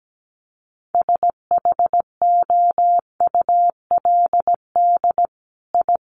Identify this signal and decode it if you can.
Morse code, heard as a keyed tone, about 17 wpm, 700 Hz. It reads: SHOULD I
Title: Sentences from Top 1300 Words 17wpm